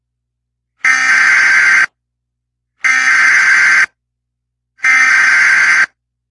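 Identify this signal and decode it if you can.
Emergency Alert System (SAME) tones (header bursts), heard as harsh buzzing bursts. It is heard from 0.8 to 5.9 s.
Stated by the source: Alert tones heard on ZT dispatch via the 7500 repeater, received on a scanner.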